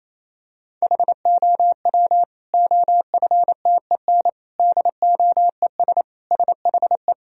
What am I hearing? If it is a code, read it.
Morse code, heard as a keyed tone, about 27 wpm, 700 Hz. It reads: HOW OFTEN DOES SHE
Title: Sentences from Top 1000 Words - Review Part 3 - 28wpm